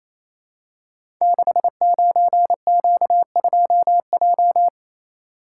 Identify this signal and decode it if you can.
Morse code, heard as a keyed tone, about 29 wpm, 700 Hz. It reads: 69Q2J
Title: Code Groups - 5 Letter:Numbers - 28wpm